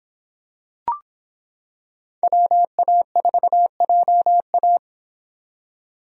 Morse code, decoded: WA4JA

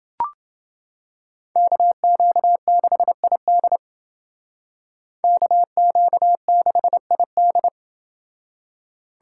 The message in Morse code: KQ6ID KQ6ID

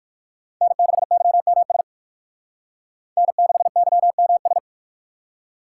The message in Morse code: N6YMS N6YMS

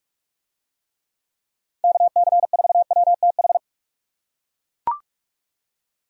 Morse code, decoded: KC4WTH

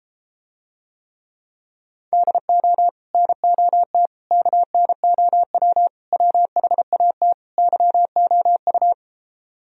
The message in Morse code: DO NOT KNOW WHAT YOU

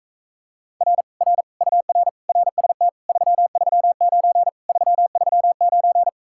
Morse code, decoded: R R UR RST 339 339

